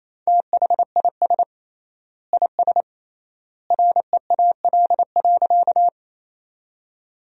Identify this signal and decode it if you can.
Morse code, heard as a keyed tone, about 28 wpm, 700 Hz. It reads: THIS IS REAL.
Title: 3x Word-Spacing - Sentences from Top 300 Words - 28wpm